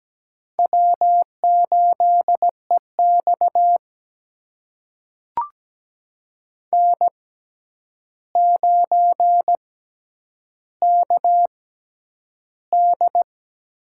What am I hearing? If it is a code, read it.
Morse code, heard as a keyed tone, about 17 wpm, spaced as for 9 wpm, 700 Hz. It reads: W8EX N9KD
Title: USA Call Signs - 6x Character Spacing - 17wpm